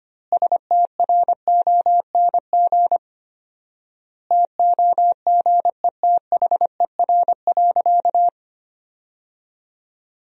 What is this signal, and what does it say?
Morse code, heard as a keyed tone, about 25 wpm, 700 Hz. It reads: STRONG TOGETHER.
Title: 4x Word-Spacing - Sentences from Top 500 Words - 25wpm